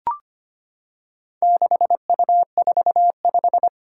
Morse code, heard as a keyed tone, about 25 wpm, 700 Hz. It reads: 6U45
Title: Four Challenging Letter-Number Combinations 25wpm